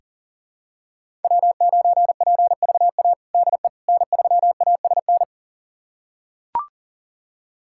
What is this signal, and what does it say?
Morse code, heard as a keyed tone, about 40 wpm, 700 Hz. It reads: W9PVA DE N3ASN
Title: USA Callsign DE Callsign - 5 Characters - 40wpm